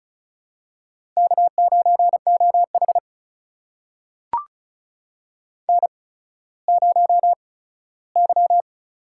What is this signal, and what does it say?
Morse code, heard as a keyed tone, about 35 wpm, spaced as for 17 wpm, 700 Hz. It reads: K9OH N0Y
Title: USA Call Signs - 4 Characters - 8x Character Spacing - 35wpm